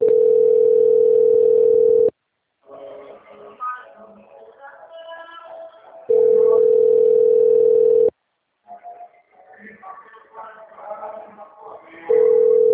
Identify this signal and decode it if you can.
Ringback tone (North America), 440 + 480 Hz, with speech.
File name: out- - - - - .3476